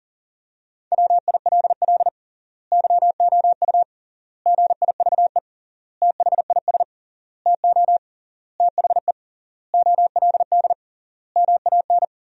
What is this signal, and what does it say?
Morse code, heard as a keyed tone, about 40 wpm, 700 Hz. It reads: WILL YOU GIVE THIS TO THE OLD MAN
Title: 3x Word-Spacing - Sentences from Top 200 Words - 40wpm